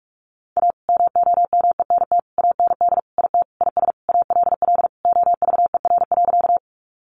Morse code, heard as a keyed tone, about 44 wpm, 700 Hz. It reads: A MOMENT AND IT IS ALL OVER.
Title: Sentences from Top 1000 Words 45wpm